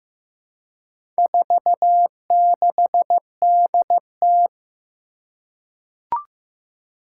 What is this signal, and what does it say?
Morse code, heard as a keyed tone, about 15 wpm, 700 Hz. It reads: 46DT
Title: Four Challenging Letter-Number Combinations 15wpm